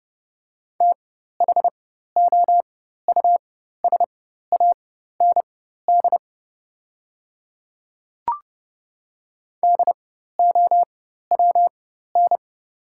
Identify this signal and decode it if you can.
Morse code, heard as a keyed tone, about 30 wpm, spaced as for 14 wpm, 700 Hz. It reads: THOUSAND DOWN
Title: 4x Character Spacing - Top 500 Words - 30wpm